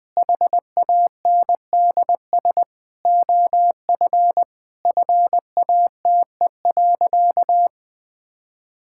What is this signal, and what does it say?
Morse code, heard as a keyed tone, about 20 wpm, 700 Hz. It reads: HANDS OF FATE.
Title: Sentences with 4-Word Phrases - 20wpm